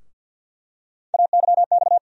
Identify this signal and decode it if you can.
Morse code, heard as a keyed tone, about 52 wpm, 700 Hz. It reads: AYX
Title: USA Call Sign Postfixes - 3 Letters 50wpm